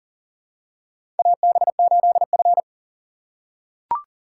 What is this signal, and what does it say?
Morse code, heard as a keyed tone, about 39 wpm, 700 Hz. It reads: AB8F
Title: USA Call Signs - 4 Characters - 3x Character Spacing - 40wpm